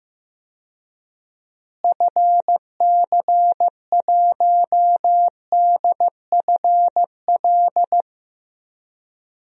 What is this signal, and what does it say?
Morse code, heard as a keyed tone, about 15 wpm, 700 Hz. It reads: FC1DFL
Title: DXCC-100 Call Signs - 6 Characters - Repeated 2x - 15wpm